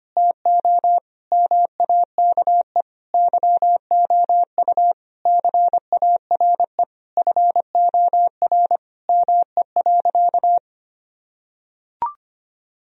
Morse code, heard as a keyed tone, about 25 wpm, 700 Hz. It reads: TO MAKE YOU CARE FOR ME.
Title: Sentences from Top 1000 Words - Review Part 1 - 25wpm